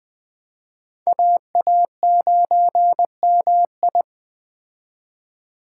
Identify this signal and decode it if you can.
Morse code, heard as a keyed tone, about 20 wpm, 700 Hz. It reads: AA9MI
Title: Speed Racing - USA 2x2 Call Signs 17wpm